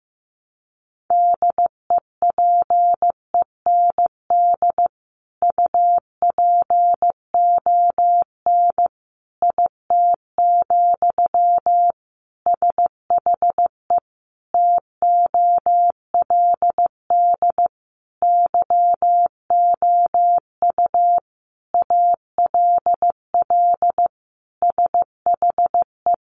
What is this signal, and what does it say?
Morse code, heard as a keyed tone, about 15 wpm, 700 Hz. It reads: DEPEND UPON IT, SHE TOLD YOU ALL SHE
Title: Sentences from Top 1300 Words 15wpm